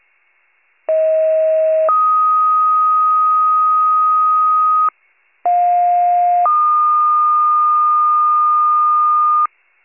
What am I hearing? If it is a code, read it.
Two-tone page, heard as a one-second tone, then a three-second tone, about 640 then 1250 Hz, about 710 then 1220 Hz. It is heard from 0.9 to 9.5 s.